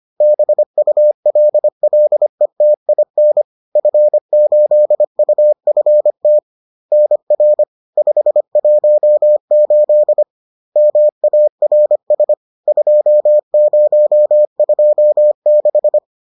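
Morse code, heard as a keyed tone, about 25 wpm, 600 Hz. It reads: BULLETIN F8UFT NR 518 MARS 2026
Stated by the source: Bulletin sent in CW.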